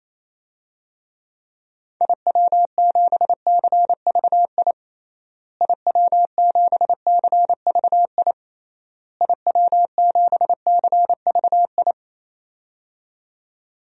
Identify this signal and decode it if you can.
Morse code, heard as a keyed tone, about 28 wpm, 700 Hz. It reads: IW7CVI IW7CVI IW7CVI
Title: DXCC-100 Call Signs - 6 Characters - Repeated 3x - 28wpm